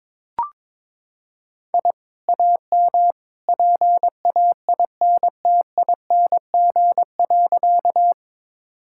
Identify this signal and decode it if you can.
Morse code, heard as a keyed tone, about 21 wpm, 700 Hz. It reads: I AM PAINTING.
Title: Sentences from Top 1000 Words - Review Part 6 - 22wpm